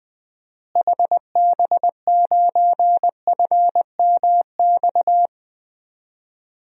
Morse code, heard as a keyed tone, about 20 wpm, 700 Hz. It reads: HB9FMX
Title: DXCC-100 Call Signs - 6 Characters - Repeated 2x - 20wpm